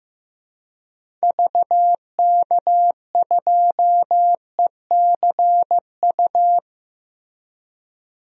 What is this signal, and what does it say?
Morse code, heard as a keyed tone, about 16 wpm, 700 Hz. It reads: VK2ECU
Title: DXCC-100 Call Signs - 6 Characters - Repeated 3x - 15wpm